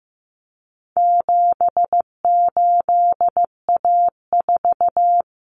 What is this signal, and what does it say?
Morse code, heard as a keyed tone, about 15 wpm, 700 Hz. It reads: 78A4
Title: Common Cut Numbers - 4-Digit 15wpm